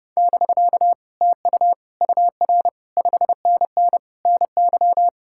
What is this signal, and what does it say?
Morse code, heard as a keyed tone, about 40 wpm, 700 Hz. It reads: <BK> TU UR 5NN NY